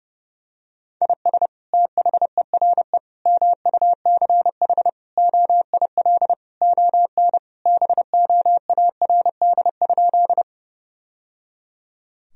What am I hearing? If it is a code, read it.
Morse code, heard as a keyed tone, about 30 wpm, 700 Hz. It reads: IS THERE MUCH OIL ON BOARD?
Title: Sentences from Top 1000 Words - Review Part 4 - 30wpm